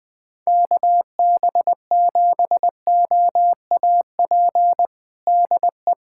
Morse code, heard as a keyed tone, about 20 wpm, 700 Hz. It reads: KB7OAP DE
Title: USA Callsign DE Callsign - 6 Characters - 20wpm